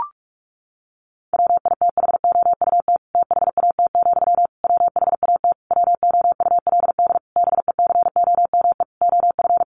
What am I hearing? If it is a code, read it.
Morse code, heard as a keyed tone, about 44 wpm, 700 Hz. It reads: WITHOUT THAT, WHAT WOULD BECOME OF
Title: Sentences from Top 600 Words 45wpm